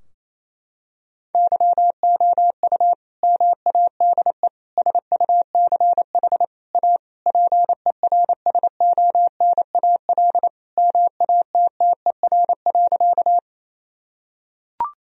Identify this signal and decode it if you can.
Morse code, heard as a keyed tone, about 28 wpm, 700 Hz. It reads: YOU MADE SUCH A PERSONAL MATTER.
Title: Sentences from Top 1000 Words - Review Part 4 - 28wpm